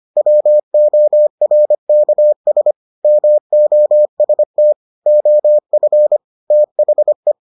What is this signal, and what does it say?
Morse code, heard as a keyed tone, about 25 wpm, 600 Hz. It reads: WORKS MOST OF THE